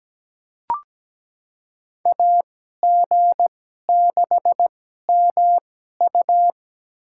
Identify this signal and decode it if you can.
Morse code, heard as a keyed tone, about 17 wpm, spaced as for 12 wpm, 700 Hz. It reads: AG6MU
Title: USA Call Signs - 5 Characters - 2x Character Spacing - 17wpm